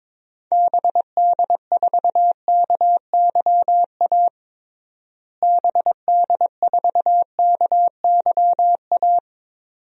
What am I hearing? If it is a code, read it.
Morse code, heard as a keyed tone, about 22 wpm, 700 Hz. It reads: BD4KYA BD4KYA
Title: DXCC-100 Call Signs - 6 Characters - Repeated 2x - 22wpm